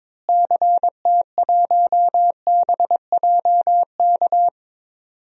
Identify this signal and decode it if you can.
Morse code, heard as a keyed tone, about 22 wpm, 700 Hz. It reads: CT1BJK
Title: DXCC-100 Call Signs - 6 Characters - Repeated 2x - 22wpm